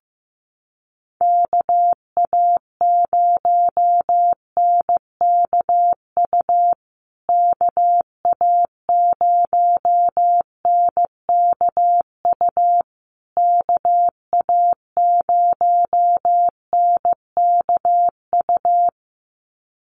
Morse code, heard as a keyed tone, about 15 wpm, 700 Hz. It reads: KA0NKU KA0NKU KA0NKU
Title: USA Call Signs Repeated 3x 15wpm